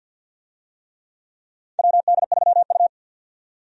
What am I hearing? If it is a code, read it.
Morse code, heard as a keyed tone, about 52 wpm, 700 Hz. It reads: WD3U
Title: USA Call Signs - 4 Characters - 50wpm